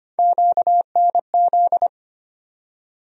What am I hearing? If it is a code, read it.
Morse code, heard as a keyed tone, about 25 wpm, 700 Hz. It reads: QNZ
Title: USA Call Sign Postfixes - 3 Letters 25wpm